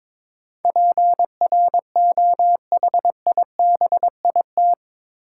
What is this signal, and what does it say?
Morse code, heard as a keyed tone, about 22 wpm, 700 Hz. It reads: PROHIBIT
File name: practice / 22wpm / Eight Letter Words - 22wpm